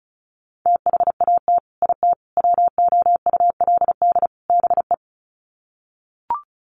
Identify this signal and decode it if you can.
Morse code, heard as a keyed tone, about 34 wpm, 700 Hz. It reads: THAT IT WOULD BE
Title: Sets of 4 Words 35wpm